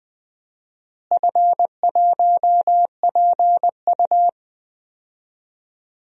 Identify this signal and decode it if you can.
Morse code, heard as a keyed tone, about 20 wpm, 700 Hz. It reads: F1PU
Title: DXCC-100 Call Signs - 4 Characters - Repeated 3x - 20wpm